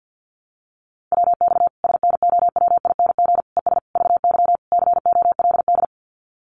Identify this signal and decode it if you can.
Morse code, heard as a keyed tone, about 50 wpm, 700 Hz. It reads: WX SNOWING ES VY COLD
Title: Common QSO Phrases 50wpm